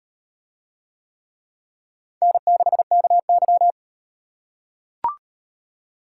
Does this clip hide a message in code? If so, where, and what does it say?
Yes: Morse code from 2.2 to 3.7 s, heard as a keyed tone, about 38 wpm, 700 Hz. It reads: N6KY